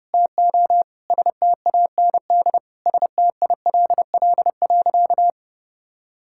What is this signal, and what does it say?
Morse code, heard as a keyed tone, about 32 wpm, 700 Hz. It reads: TO STAND STILL.
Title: Sentences from Top 1000 Words - Review Part 6 - 30wpm